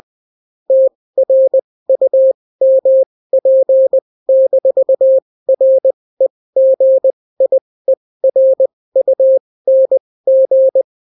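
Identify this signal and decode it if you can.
Morse code, heard as a keyed tone, about 20 wpm, spaced as for 16 wpm, 530 Hz. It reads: TRUMP-REGIERUNG
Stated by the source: Daily Morse weather and top stories at 16 wpm, clean without QRN.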